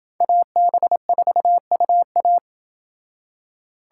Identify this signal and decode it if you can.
Morse code, heard as a keyed tone, about 28 wpm, 700 Hz. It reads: AB4UA